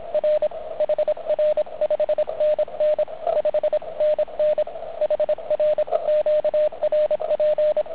Morse code, heard as a keyed tone, about 26 wpm, 610 Hz, with radio noise. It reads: R HR 5NN 5NN HR QRP